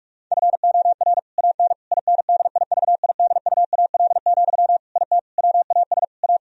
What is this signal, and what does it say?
Morse code, heard as a keyed tone, about 46 wpm, 700 Hz. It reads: FOR AN INDIVIDUAL, IT WAS A
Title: Sentences from Top 1000 Words - Review Part 4 - 45wpm